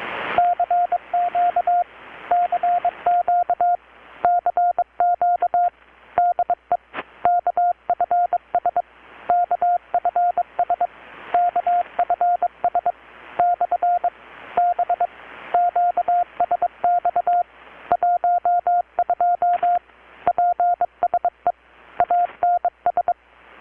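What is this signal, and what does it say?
Morse code, heard as a keyed tone, about 23 wpm, 720 Hz, with radio noise. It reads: CQ CQ CQ DE KFS KFS KFS / B QSX 12 PSE ANS
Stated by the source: Morse from KPH's Night of Nights XX maritime radiotelegraphy event, received in Chiapas, Mexico about 3,800 km away on a Kenwood R-600 with a grounded 10 m long-wire antenna.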